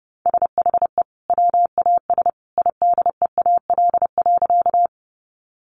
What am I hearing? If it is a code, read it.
Morse code, heard as a keyed tone, about 30 wpm, 700 Hz. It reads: SHE WAS IDEAL.
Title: Sentences from Top 1400 Words 30wpm